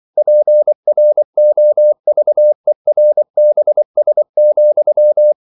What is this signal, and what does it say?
Morse code, heard as a keyed tone, about 24 wpm, 600 Hz. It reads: PROVERBS,
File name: CW / Practice / EnglishQuotations0224